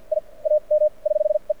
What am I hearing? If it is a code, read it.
Morse code, heard as a keyed tone, about 53 wpm, spaced as for 39 wpm, 600 Hz. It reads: I AM <HH>E
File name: archived / CRSnet